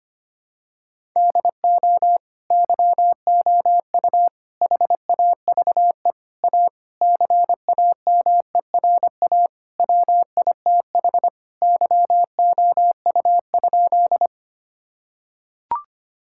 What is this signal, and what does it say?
Morse code, heard as a keyed tone, about 25 wpm, 700 Hz. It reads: DO YOU HAVE A CAMERA WITH YOU?
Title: Sentences from Top 1000 Words - Review Part 5 - 25wpm